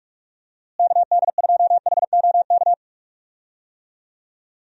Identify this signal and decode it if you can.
Morse code, heard as a keyed tone, about 46 wpm, 700 Hz. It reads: KD2HOK
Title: USA Call Signs - 6 Characters - Repeated 2x - 45wpm